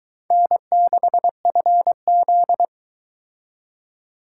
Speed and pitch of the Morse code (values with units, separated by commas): 23 wpm, 700 Hz